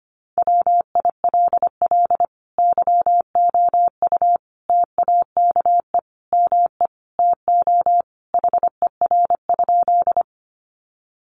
Morse code, heard as a keyed tone, about 25 wpm, 700 Hz. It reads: WILL YOU TAKE ME TO HER?
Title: Sentences from Top 200 Words 25wpm